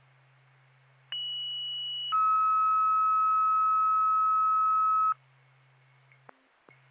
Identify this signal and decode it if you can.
Two-tone page, heard as a one-second tone, then a three-second tone, about 2950 then 1290 Hz. It is heard from 1.1 to 5.1 s.